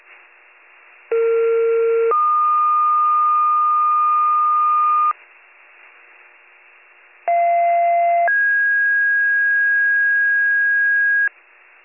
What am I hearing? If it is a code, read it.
Two-tone page, heard as a one-second tone, then a three-second tone, about 460 then 1220 Hz, about 690 then 1650 Hz. It is heard from 1.1 to 11.3 s.